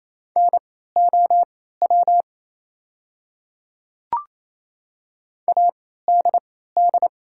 Morse code, heard as a keyed tone, about 28 wpm, spaced as for 16 wpm, 700 Hz. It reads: NOW ADD
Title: 3x Character Spacing - Top 200 Words - 28wpm